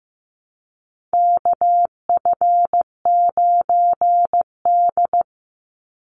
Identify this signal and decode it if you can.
Morse code, heard as a keyed tone, about 15 wpm, 700 Hz. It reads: KF9D
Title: Call Signs Repeated 2x - 15wpm